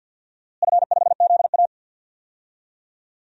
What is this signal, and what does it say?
Morse code, heard as a keyed tone, about 53 wpm, 700 Hz. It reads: F5ZA
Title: DXCC-100 Call Signs - 4 Characters - Repeated 2x - 50wpm